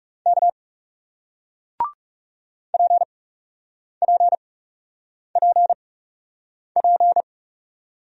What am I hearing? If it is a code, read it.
Morse code, heard as a keyed tone, about 40 wpm, spaced as for 8 wpm, 700 Hz. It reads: K PPPP